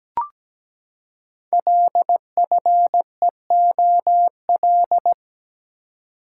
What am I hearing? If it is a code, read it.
Morse code, heard as a keyed tone, about 17 wpm, 700 Hz. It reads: LFEOL
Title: Code Groups - 5 Letters - 17wpm